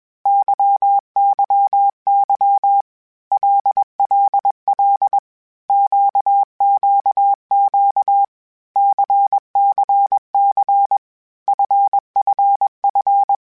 Morse code, heard as a keyed tone, about 21 wpm, 800 Hz. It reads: YYY LLL QQQ CCC FFF
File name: Basic Sets / Basic Letter Groups Collective7 / LFCYQ@20